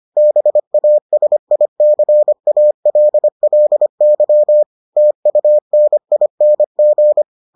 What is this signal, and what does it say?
Morse code, heard as a keyed tone, about 25 wpm, 600 Hz. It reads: BASICALLY TUNING